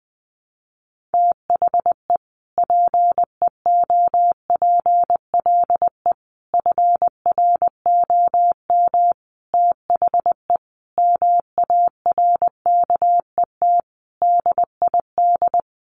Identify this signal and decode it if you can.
Morse code, heard as a keyed tone, about 20 wpm, 700 Hz. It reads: THE PEOPLE FROM THE MARKET DID
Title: Sentences from Top 600 Words 20wpm